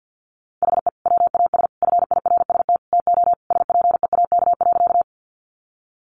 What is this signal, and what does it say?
Morse code, heard as a keyed tone, about 50 wpm, 700 Hz. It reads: HE WAS FIRST TO SPEAK.
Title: Sentences from Top 1100 Words 50wpm